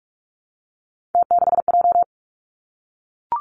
Morse code, heard as a keyed tone, about 46 wpm, 700 Hz. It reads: T6J
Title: Three Challenging Letter-Number Combinations 45wpm